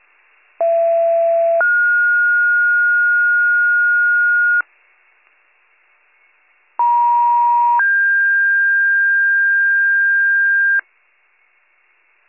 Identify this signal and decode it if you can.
Two-tone page, heard as a one-second tone, then a three-second tone, about 670 then 1430 Hz, about 950 then 1600 Hz. It is heard from 0.6 to 10.8 s.